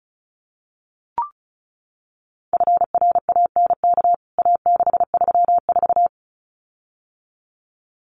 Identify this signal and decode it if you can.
Morse code, heard as a keyed tone, about 34 wpm, 700 Hz. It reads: FRANK A634
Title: CWT Exchange 35wpm